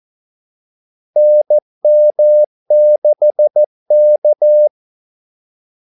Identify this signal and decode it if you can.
Morse code, heard as a keyed tone, about 14 wpm, 600 Hz. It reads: NM6K
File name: Callsigns-14wpm